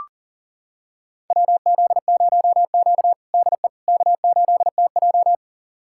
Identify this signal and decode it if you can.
Morse code, heard as a keyed tone, about 37 wpm, 700 Hz. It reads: WZ0Q DE K8TJ
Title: USA Callsign DE Callsign - 4 Characters - 40wpm